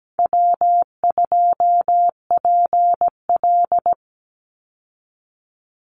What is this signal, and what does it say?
Morse code, heard as a keyed tone, about 17 wpm, 700 Hz. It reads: W2PL
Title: Easy Call Signs Repeated 2x - 17wpm